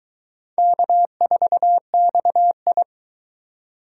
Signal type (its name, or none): Morse code